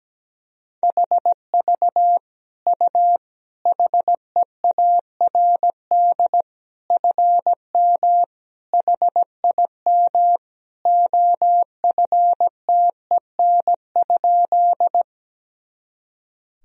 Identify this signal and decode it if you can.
Morse code, heard as a keyed tone, about 17 wpm, 700 Hz. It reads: HV U HEARD FM HIM OFTEN?
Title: Sentences from Top 400 Words and Abbreviations 17wpm